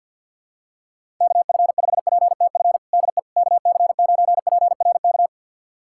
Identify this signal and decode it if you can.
Morse code, heard as a keyed tone, about 47 wpm, 700 Hz. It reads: KF5PTF DE KC9PRK
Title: USA Callsign DE Callsign - 6 Characters - 50wpm